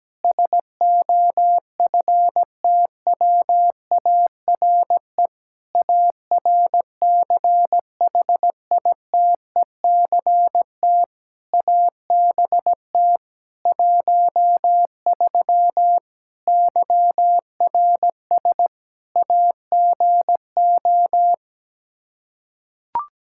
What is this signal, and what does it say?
Morse code, heard as a keyed tone, about 17 wpm, 700 Hz. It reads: SOFTWARE ARCHITECT ABT 13 YRS AGO
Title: Practice QSO - Exchange 3 - More Info - 17wpm